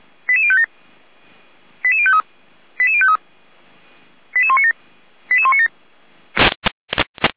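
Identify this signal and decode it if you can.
Five-tone selective call (ZVEI), heard as quick runs of five tones, tones of about 70 ms. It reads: 88057 88053 88053 88197 88197
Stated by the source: Five-tone ZVEI selective call received on an SDR by feuerwehr-orlamuende.org (Feuerwehr Orlamünde).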